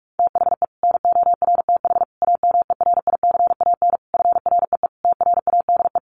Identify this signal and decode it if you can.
Morse code, heard as a keyed tone, about 44 wpm, 700 Hz. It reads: THE NORTH AMERICAN FREE TRADE
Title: Sets of 5 Words Encore 45wpm